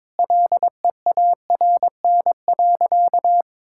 Morse code, heard as a keyed tone, about 21 wpm, 700 Hz. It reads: LEARN.
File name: practice / 22wpm / Sentences with 3-Word Phrases - 22wpm